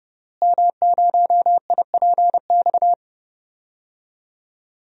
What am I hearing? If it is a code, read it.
Morse code, heard as a keyed tone, about 30 wpm, 700 Hz. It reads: M0IPX
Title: DXCC-100 Call Signs - 5 Characters - Repeated 2x - 30wpm